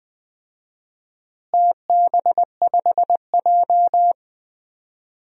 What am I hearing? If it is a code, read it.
Morse code, heard as a keyed tone, about 21 wpm, 700 Hz. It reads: TB5J
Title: DXCC-100 Call Signs - 4 Characters - Repeated 2x - 20wpm